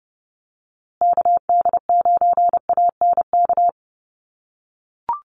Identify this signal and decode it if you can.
Morse code, heard as a keyed tone, about 30 wpm, 700 Hz. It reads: KD9ANK
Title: USA Call Signs Repeated 3x 30wpm